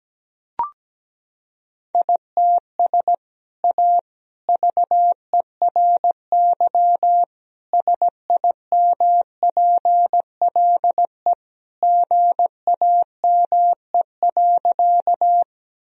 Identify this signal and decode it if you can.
Morse code, heard as a keyed tone, about 17 wpm, 700 Hz. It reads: ITS A VERY SIMPLE GAME.
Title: Sentences from Top 1000 Words - Review Part 3 - 17wpm